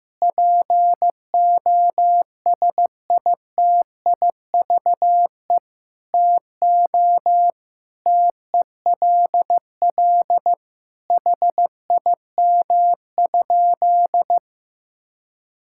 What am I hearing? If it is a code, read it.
Morse code, heard as a keyed tone, about 15 wpm, 700 Hz. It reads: POSITIVE TO TELL HIM?
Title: Sentences from Top 1000 Words - Review Part 5 - 15wpm